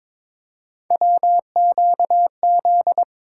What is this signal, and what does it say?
Morse code, heard as a keyed tone, about 22 wpm, 700 Hz. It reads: WQZ
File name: practice / 22wpm / Three Letter-Number Combinations - 22wpm